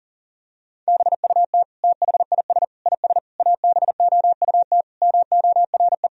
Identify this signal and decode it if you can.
Morse code, heard as a keyed tone, about 40 wpm, 700 Hz. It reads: BUT THIS IS ABOUT MORE
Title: Sentences from Top 1000 Words - Review Part 3 - 40wpm